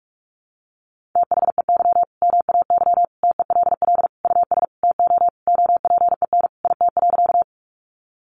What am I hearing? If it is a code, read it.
Morse code, heard as a keyed tone, about 44 wpm, 700 Hz. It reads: THEY MAY TELL US TO OPEN IT.
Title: Sentences from Top 300 Words 45wpm